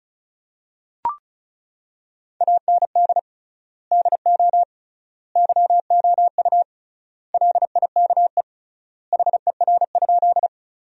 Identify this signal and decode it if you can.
Morse code, heard as a keyed tone, about 36 wpm, 700 Hz. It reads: AND DO YOU LIKE HER?
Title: 3x Word-Spacing - Sentences from Top 100 Words - 35wpm